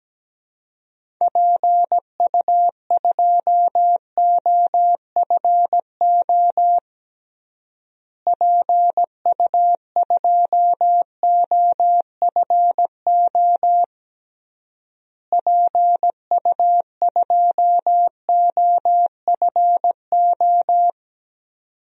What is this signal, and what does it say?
Morse code, heard as a keyed tone, about 17 wpm, 700 Hz. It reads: PU2OFO PU2OFO PU2OFO
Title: DXCC-100 Call Signs - 6 Characters - Repeated 3x - 17wpm